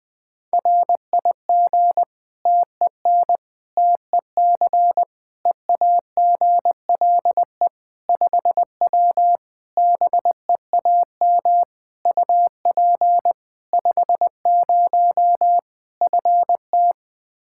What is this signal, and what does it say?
Morse code, heard as a keyed tone, about 21 wpm, 700 Hz. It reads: RIG TEN TEC EAGLE 5W BEAM UP 50 FT
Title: Practice QSO - Exchange 2 - Info - 20wpm